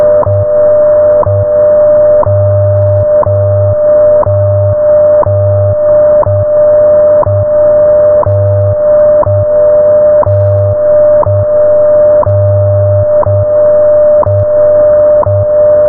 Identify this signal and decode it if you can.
Radio time signal.
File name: Radio / SDR Audio / audio z 19-56-20 17-09-2022